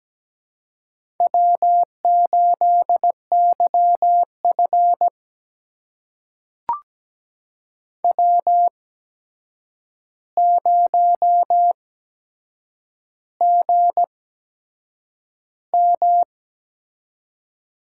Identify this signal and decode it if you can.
Morse code, heard as a keyed tone, about 17 wpm, spaced as for 7 wpm, 700 Hz. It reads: W8YF W0GM
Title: USA Call Signs - 4 Characters - 8x Character Spacing - 17wpm